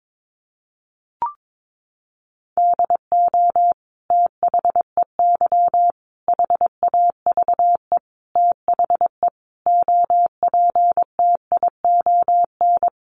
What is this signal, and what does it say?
Morse code, heard as a keyed tone, about 22 wpm, 700 Hz. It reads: DO THEY HAVE THE OPTION
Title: Sentences from Top 1300 Words 22wpm